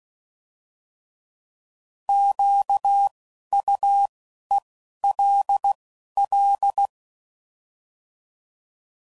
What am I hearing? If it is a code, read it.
Morse code, heard as a keyed tone, about 16 wpm, 800 Hz. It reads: Q U E L L
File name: Prefixes / By First Letter / Qs